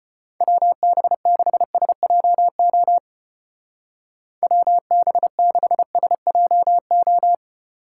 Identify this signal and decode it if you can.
Morse code, heard as a keyed tone, about 31 wpm, 700 Hz. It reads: WB6SJO WB6SJO
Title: Speed Racing - USA Call Signs 25wpm